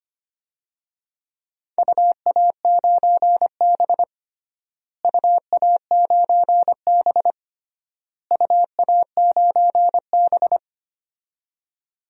Morse code, heard as a keyed tone, about 25 wpm, 700 Hz. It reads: UA9B UA9B UA9B
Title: DXCC-100 Call Signs - 4 Characters - Repeated 3x - 25wpm